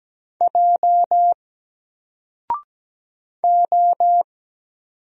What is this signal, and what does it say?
Morse code, heard as a keyed tone, about 17 wpm, 700 Hz. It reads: J O
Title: Single Letter-Number Alt 17wpm